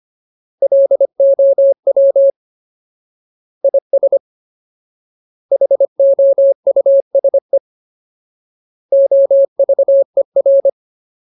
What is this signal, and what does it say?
Morse code, heard as a keyed tone, about 25 wpm, 550 Hz. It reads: LOW IS HOUSE OVER
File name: Ultimate Word List / 25WPM Ultimate